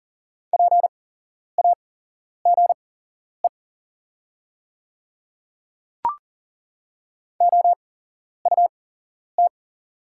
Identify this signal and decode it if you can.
Morse code, heard as a keyed tone, about 41 wpm, spaced as for 11 wpm, 700 Hz. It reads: PAGE OUT
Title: 8x Character Spacing - Top 300 Words - 40wpm